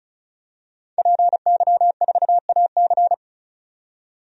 Morse code, heard as a keyed tone, about 36 wpm, 700 Hz. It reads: PY4AC
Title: DXCC-100 Call Signs - 5 Characters - 35wpm